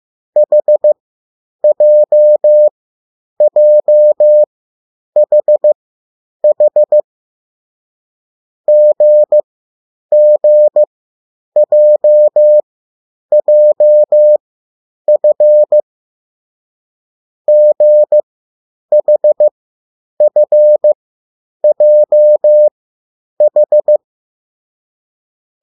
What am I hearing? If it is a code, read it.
Morse code, heard as a keyed tone, about 15 wpm, spaced as for 9 wpm, 600 Hz. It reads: HJJHH GGJJF GHFJH
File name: urok 1